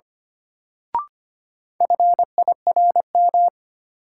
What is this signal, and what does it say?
Morse code, heard as a keyed tone, about 25 wpm, 700 Hz. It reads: FIRM
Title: Sentences from Top 1400 Words - New Word Drill - 25wpm